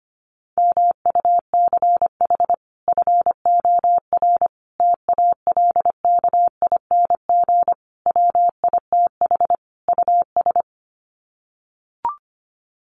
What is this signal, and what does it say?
Morse code, heard as a keyed tone, about 25 wpm, 700 Hz. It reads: MUCH FOR TALKING WITH US
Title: Sets of 5 Words Encore 25wpm